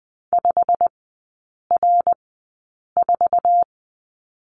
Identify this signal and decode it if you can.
Morse code, heard as a keyed tone, about 20 wpm, spaced as for 8 wpm, 700 Hz. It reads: 5R4
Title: 087 Lesson 15e - Number 5 - Three Characters 20wpm